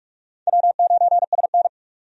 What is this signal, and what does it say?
Morse code, heard as a keyed tone, about 46 wpm, 700 Hz. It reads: W9SN
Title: USA Call Signs - 4 Characters - 45wpm